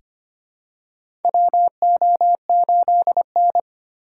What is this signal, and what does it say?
Morse code, heard as a keyed tone, about 25 wpm, 700 Hz. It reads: WO8N
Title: USA Call Signs - 4 Characters - 3x Character Spacing - 25wpm